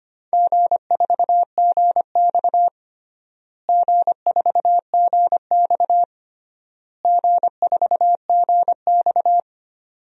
Morse code, heard as a keyed tone, about 25 wpm, 700 Hz. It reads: G4GX G4GX G4GX
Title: DXCC-100 Call Signs - 4 Characters - Repeated 3x - 25wpm